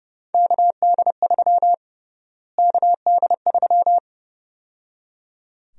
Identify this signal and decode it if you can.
Morse code, heard as a keyed tone, about 32 wpm, 700 Hz. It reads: KD3 KD3
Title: USA Call Sign Prefixes - Repeated 2x - 30wpm